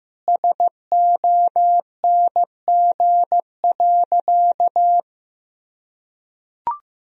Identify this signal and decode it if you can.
Morse code, heard as a keyed tone, about 15 wpm, 700 Hz. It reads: SONG.
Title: Sentences from Top 1000 Words - Review Part 2 - 15wpm